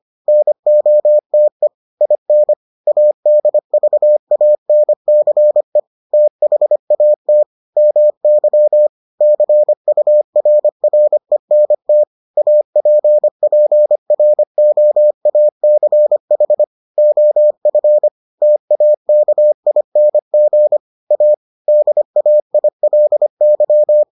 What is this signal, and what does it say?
Morse code, heard as a keyed tone, about 25 wpm, 600 Hz. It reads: NOTE IN ADVANCE THAT MY CURRENT APPROACH OF TAKING A DAILY